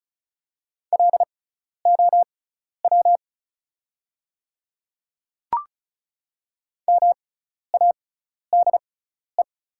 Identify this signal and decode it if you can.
Morse code, heard as a keyed tone, about 36 wpm, spaced as for 12 wpm, 690 Hz. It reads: LOW MADE
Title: 6x Character Spacing - Top 200 Words - 35wpm